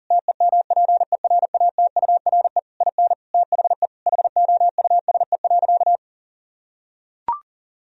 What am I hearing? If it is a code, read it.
Morse code, heard as a keyed tone, about 37 wpm, 700 Hz. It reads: TEMPERATURE IN THE HOUSE.
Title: Sentences from Top 1000 Words - Review Part 4 - 40wpm